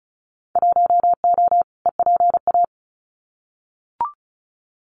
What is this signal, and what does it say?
Morse code, heard as a keyed tone, about 34 wpm, 700 Hz. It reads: 1O EPA
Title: Winter Field Day Exchanges 35wpm